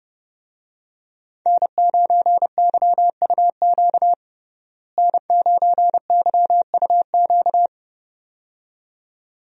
Morse code, heard as a keyed tone, about 32 wpm, 700 Hz. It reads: N9YUQ N9YUQ
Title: USA Call Signs - 5 Characters - Repeated 2x - 30wpm